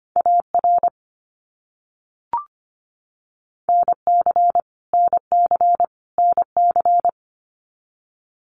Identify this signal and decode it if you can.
Morse code, heard as a keyed tone, about 25 wpm, 700 Hz. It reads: AR NC NC NC
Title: US States Repeated 3x 25wpm